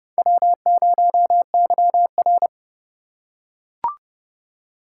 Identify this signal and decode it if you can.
Morse code, heard as a keyed tone, about 30 wpm, 700 Hz. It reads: W0YR
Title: K1USN SST QSO Practice - 30wpm